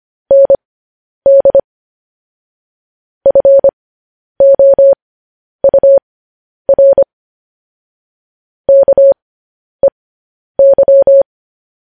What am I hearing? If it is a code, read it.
Morse code, heard as a keyed tone, about 25 wpm, spaced as for 10 wpm, 550 Hz. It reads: ND FOUR KEY
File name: podcast / News headlines-1x-25x10